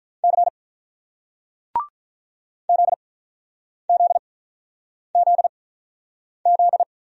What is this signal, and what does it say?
Morse code, heard as a keyed tone, about 48 wpm, spaced as for 9 wpm, 700 Hz. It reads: C ZZZZ